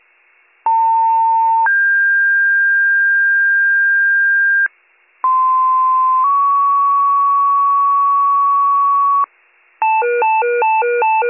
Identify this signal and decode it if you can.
Two-tone page, heard as a one-second tone, then a three-second tone, about 900 then 1600 Hz, about 1040 then 1120 Hz. It is heard from 0.7 to 9.2 s.